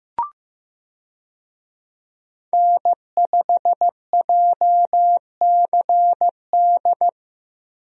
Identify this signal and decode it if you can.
Morse code, heard as a keyed tone, about 15 wpm, 700 Hz. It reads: N5JCD
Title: USA Call Signs - 5 Characters - Repeated 2x - 15wpm